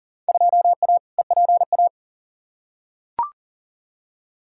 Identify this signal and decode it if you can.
Morse code, heard as a keyed tone, about 40 wpm, 700 Hz. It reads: 2A EPA
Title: Field Day Exchanges - 40wpm